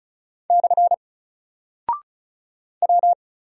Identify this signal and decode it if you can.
Morse code, heard as a keyed tone, about 34 wpm, 700 Hz. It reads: / W
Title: Single Character (Letter:Number:Punctuation:Prosign) Alt 35wpm